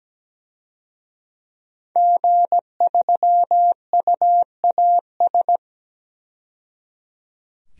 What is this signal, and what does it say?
Morse code, heard as a keyed tone, about 17 wpm, 700 Hz. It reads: G3UAS
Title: DXCC-100 Call Signs - 5 Characters - 17wpm